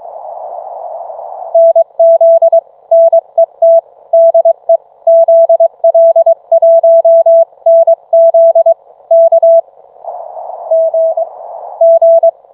Morse code, heard as a keyed tone, about 24 wpm, 660 Hz. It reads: NZ NET DE ZL1NZ K G G